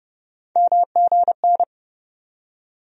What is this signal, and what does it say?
Morse code, heard as a keyed tone, about 30 wpm, 700 Hz. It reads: MGN
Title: USA Call Sign Postfixes - 3 Letters 30wpm